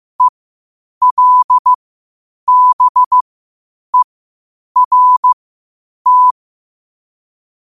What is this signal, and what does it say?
Morse code, heard as a keyed tone, about 15 wpm, spaced as for 8 wpm, 1000 Hz. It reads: ELBERT
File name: qotdfiles / 8wpm